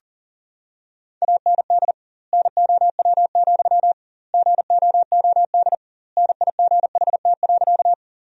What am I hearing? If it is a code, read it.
Morse code, heard as a keyed tone, about 40 wpm, 700 Hz. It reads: AND NOW, GOOD NIGHT.
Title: 2x Word-Spacing - Sentences from Top 300 Words - 40wpm